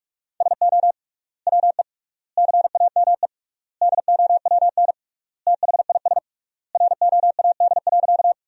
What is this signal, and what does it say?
Morse code, heard as a keyed tone, about 42 wpm, 700 Hz. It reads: SO WE CAME DOWN THIS ROAD.
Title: 3x Word-Spacing - Sentences from Top 400 Words - 45wpm